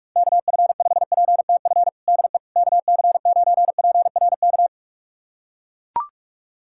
Morse code, heard as a keyed tone, about 41 wpm, 700 Hz. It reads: KF5PTF DE KC9PRK
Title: USA Callsign DE Callsign - 6 Characters - 45wpm